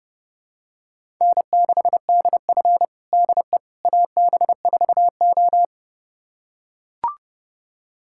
Morse code, heard as a keyed tone, about 30 wpm, 700 Hz. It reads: N6DF DE AB4O
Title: USA Callsign DE Callsign - 4 Characters - 30wpm